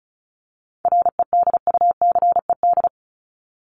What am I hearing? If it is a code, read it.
Morse code, heard as a keyed tone, about 34 wpm, 700 Hz. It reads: REDUCED
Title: ED Suffix - Top 100 Words with Base Words 35wpm